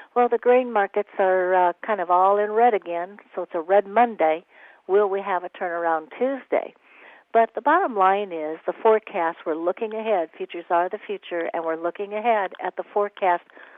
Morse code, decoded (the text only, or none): none